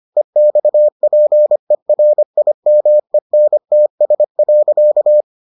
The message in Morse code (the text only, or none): EXPERIMENTS.